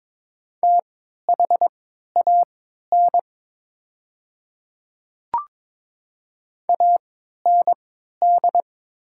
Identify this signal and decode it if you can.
Morse code, heard as a keyed tone, about 23 wpm, spaced as for 13 wpm, 700 Hz. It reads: THAN AND
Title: 3x Character Spacing - Top 100 Words - 22wpm